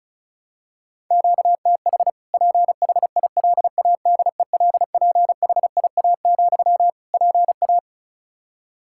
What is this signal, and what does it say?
Morse code, heard as a keyed tone, about 36 wpm, 700 Hz. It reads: QTH PHILADELPHIA, PA
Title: Practice QSO - Exchange 1 - Beginning - 35wpm